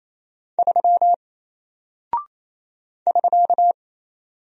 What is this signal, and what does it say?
Morse code, heard as a keyed tone, about 28 wpm, 700 Hz. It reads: 3 <SK>
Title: Single Character (Letter:Number:Punctuation:Prosign) Alt 28wpm